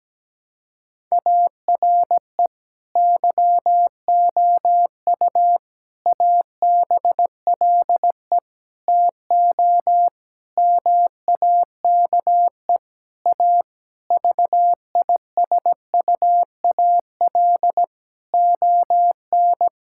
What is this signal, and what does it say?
Morse code, heard as a keyed tone, about 17 wpm, 700 Hz. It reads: ARE YOU ABLE TO MAKE A VISUAL ON
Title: Sentences from Top 1000 Words - Review Part 6 - 17wpm